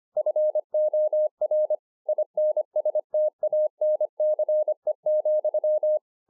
Morse code, heard as a keyed tone, about 25 wpm, 610 Hz. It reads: FOR INSTANCE,